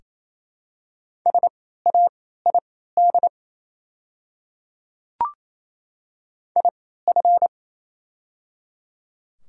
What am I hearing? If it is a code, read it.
Morse code, heard as a keyed tone, about 28 wpm, spaced as for 16 wpm, 700 Hz. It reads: SAID IF
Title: 3x Character Spacing - Top 100 Words - 28wpm